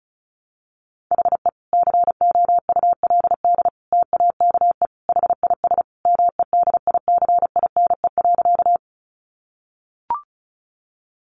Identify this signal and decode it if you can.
Morse code, heard as a keyed tone, about 34 wpm, 700 Hz. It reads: HE COULD TAKE HIS MEDICINE.
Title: Sentences from Top 900 Words 35wpm